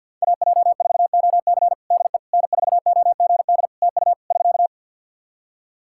Morse code, heard as a keyed tone, about 53 wpm, 700 Hz. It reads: AJ4OC DE N4OGD TU <SK>
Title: Practice QSO - Exchange 4 - Ending - 50wpm